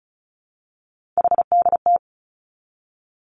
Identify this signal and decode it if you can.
Morse code, heard as a keyed tone, about 36 wpm, 700 Hz. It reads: HDT
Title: Three Challenging Letter-Number Combinations 35wpm